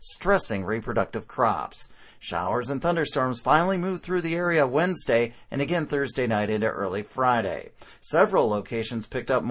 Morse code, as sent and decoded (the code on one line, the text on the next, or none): none